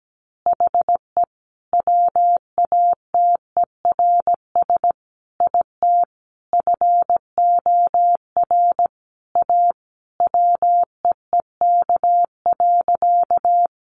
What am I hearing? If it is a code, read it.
Morse code, heard as a keyed tone, about 17 wpm, 700 Hz. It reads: HE WATERS IT FOR A WEEK.
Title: Easy Phrases 17wpm